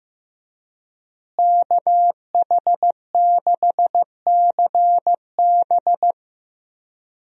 Morse code, heard as a keyed tone, about 16 wpm, 700 Hz. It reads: KH6CB
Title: USA Call Signs - 5 Characters - Repeated 2x - 15wpm